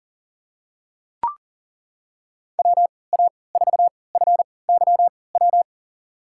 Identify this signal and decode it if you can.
Morse code, heard as a keyed tone, about 38 wpm, spaced as for 22 wpm, 700 Hz. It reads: WA4FYW
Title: USA Call Signs - 3x Character Spacing - 40wpm